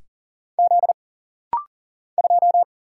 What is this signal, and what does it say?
Morse code, heard as a keyed tone, about 40 wpm, 700 Hz. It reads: Z 2
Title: Single Letter-Number Alt - Rapid-Fire 40wpm